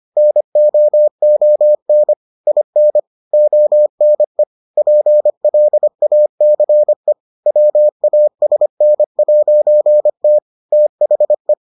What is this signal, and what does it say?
Morse code, heard as a keyed tone, about 25 wpm, 600 Hz. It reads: NOON IN ONE PLACE WASN'T THE